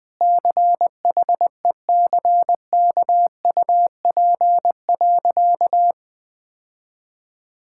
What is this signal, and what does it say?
Morse code, heard as a keyed tone, about 21 wpm, 700 Hz. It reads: CHECKUP.
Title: Easy Sentences with Hints - 20wpm